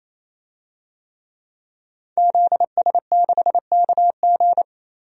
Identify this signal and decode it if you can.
Morse code, heard as a keyed tone, about 28 wpm, 700 Hz. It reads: ZS6KG
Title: DXCC-100 Call Signs - 5 Characters - 28wpm